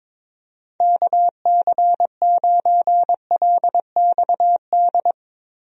Morse code, heard as a keyed tone, about 22 wpm, 700 Hz. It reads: KC9LXD
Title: USA Call Signs - 6 Characters - Repeated 3x - 22wpm